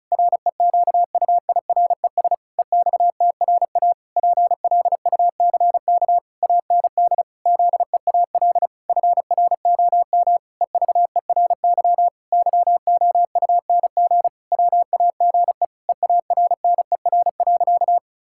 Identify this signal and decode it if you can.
Morse code, heard as a keyed tone, about 34 wpm, 700 Hz. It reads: REQUIRES EXTRA PLUCK AND ZEAL FROM EVERY YOUNG WAGE EARNER.
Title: Pangram Challenge v2 - 35wpm